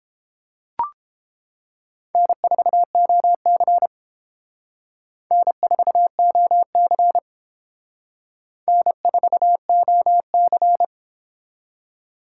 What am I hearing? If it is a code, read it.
Morse code, heard as a keyed tone, about 30 wpm, 700 Hz. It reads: N4OC N4OC N4OC